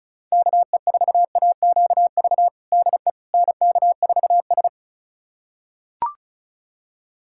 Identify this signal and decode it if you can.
Morse code, heard as a keyed tone, about 34 wpm, 700 Hz. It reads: KE4AQV DE NK4S
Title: USA Callsign DE Callsign 35wpm